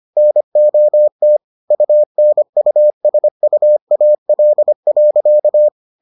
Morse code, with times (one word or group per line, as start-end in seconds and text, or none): 0.2-1.4 NOT
1.7-5.7 UNUSUAL.